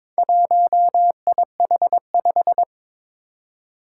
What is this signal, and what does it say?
Morse code, heard as a keyed tone, about 22 wpm, 700 Hz. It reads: 1IH5
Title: Four Challenging Letter-Number Combinations 22wpm